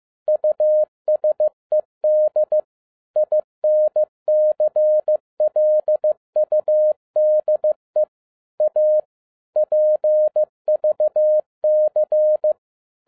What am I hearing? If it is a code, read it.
Morse code, heard as a keyed tone, about 15 wpm, 610 Hz. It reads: USED INCLUDE A PVC
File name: 2012-01-mp3 / n07-ARA JR0DIL-15wpm